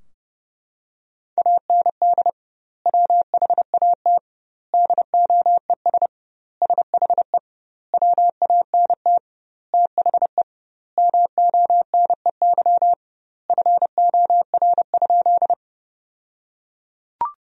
Morse code, heard as a keyed tone, about 32 wpm, 700 Hz. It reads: AND WHAT DOES SHE WANT THE MONEY FOR?
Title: 2x Word-Spacing - Sentences from Top 400 Words - 30wpm